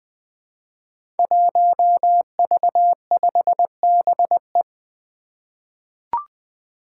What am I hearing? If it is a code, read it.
Morse code, heard as a keyed tone, about 20 wpm, 700 Hz. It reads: 1V5BE
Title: Five Challenging Letter-Number Combinations 20wpm